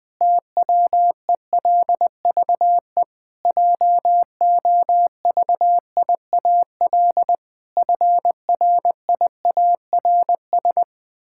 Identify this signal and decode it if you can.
Morse code, heard as a keyed tone, about 20 wpm, 700 Hz. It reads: TWELVE JOVIAL FRIARS
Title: Pangram Challenge v2 - 20wpm